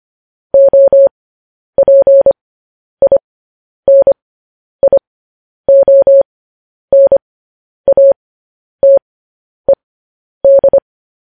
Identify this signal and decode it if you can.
Morse code, heard as a keyed tone, about 25 wpm, spaced as for 10 wpm, 550 Hz. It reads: OPINIONATED
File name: podcast / News headlines-1x-25x10